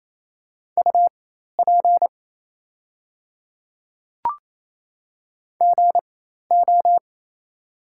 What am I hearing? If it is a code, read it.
Morse code, heard as a keyed tone, about 28 wpm, spaced as for 13 wpm, 690 Hz. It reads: UP GO
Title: 4x Character Spacing - Top 200 Words - 28wpm